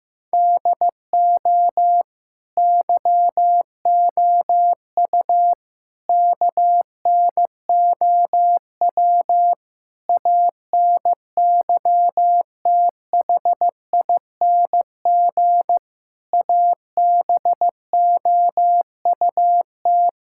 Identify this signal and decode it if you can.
Morse code, heard as a keyed tone, about 15 wpm, 700 Hz. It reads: DO YOU KNOW ANYTHING ABOUT
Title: Sentences from Top 1600 Words 15wpm